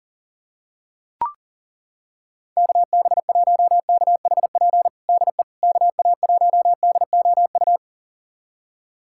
Morse code, heard as a keyed tone, about 40 wpm, 700 Hz. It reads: KB1KHP DE KA1DOU
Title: USA Callsign DE Callsign 40wpm